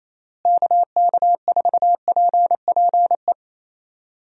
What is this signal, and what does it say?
Morse code, heard as a keyed tone, about 28 wpm, 700 Hz. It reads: KK4PPE
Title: USA Call Signs - 6 Characters - Repeated 3x - 28wpm